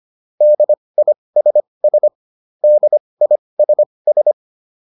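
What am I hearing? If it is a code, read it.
Morse code, heard as a keyed tone, about 25 wpm, spaced as for 20 wpm, 600 Hz. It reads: DISS DISS
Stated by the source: Advanced Morse practice drill.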